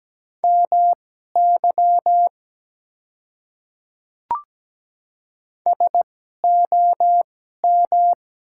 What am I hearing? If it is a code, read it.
Morse code, heard as a keyed tone, about 17 wpm, spaced as for 12 wpm, 700 Hz. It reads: MY SOM
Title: 2x Character Spacing - Top 200 Words - 17wpm